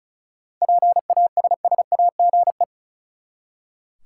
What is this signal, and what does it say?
Morse code, heard as a keyed tone, about 34 wpm, 700 Hz. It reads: PASSAGE
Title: Seven Letter Words - 35wpm